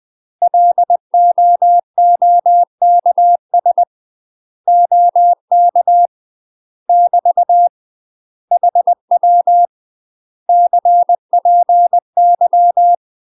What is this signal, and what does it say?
Morse code, heard as a keyed tone, about 20 wpm, 700 Hz. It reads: LOOKS OK <BT> HW CPY
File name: UK MASTER QSOs TRAINING / 20WPM / UK MASTER QSOs 0010